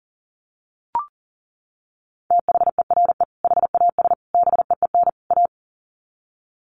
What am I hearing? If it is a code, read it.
Morse code, heard as a keyed tone, about 40 wpm, 700 Hz. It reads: THERE HAS BEEN A
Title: Sets of 4 Words 40wpm